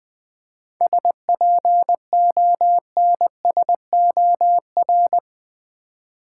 Morse code, heard as a keyed tone, about 21 wpm, 700 Hz. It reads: SPONSOR
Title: 7 Letter Words - First 3 Letters Given - 20wpm